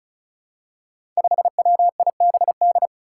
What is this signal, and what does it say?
Morse code, heard as a keyed tone, about 36 wpm, 700 Hz. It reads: 5WIBD
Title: Five Challenging Letter-Number Combinations 35wpm